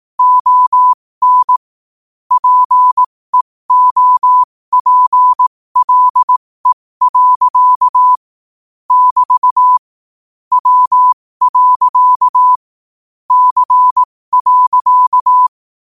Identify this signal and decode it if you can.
Morse code, heard as a keyed tone, about 18 wpm, spaced as for 15 wpm, 1000 Hz. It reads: ON PEOPLE. <BT> W. C.
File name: qotdfiles / 15wpm